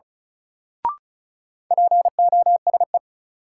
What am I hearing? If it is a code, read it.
Morse code, heard as a keyed tone, about 36 wpm, 700 Hz. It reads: POSE
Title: Sentences from Top 1000 Words - Review Part 2 - New Word Drill - 35wpm